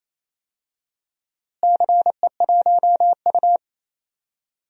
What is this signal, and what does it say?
Morse code, heard as a keyed tone, about 28 wpm, 700 Hz. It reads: CE1U
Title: DXCC-100 Call Signs - 4 Characters - Repeated 3x - 28wpm